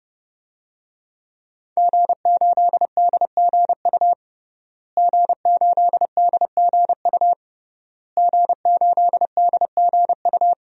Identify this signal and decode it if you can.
Morse code, heard as a keyed tone, about 30 wpm, 700 Hz. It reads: G8DGU G8DGU G8DGU
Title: DXCC-100 Call Signs - 5 Characters - Repeated 3x - 30wpm